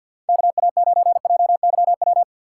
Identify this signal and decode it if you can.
Morse code, heard as a keyed tone, about 53 wpm, 700 Hz. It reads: KA9JYW
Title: USA Call Signs - 6x Character Spacing - 50wpm